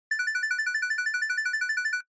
There is no code.